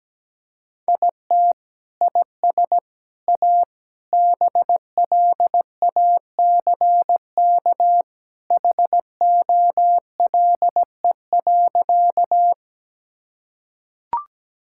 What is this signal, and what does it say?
Morse code, heard as a keyed tone, about 17 wpm, 700 Hz. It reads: IT IS A BLACK HOLE.
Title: Sentences from Top 1600 Words 17wpm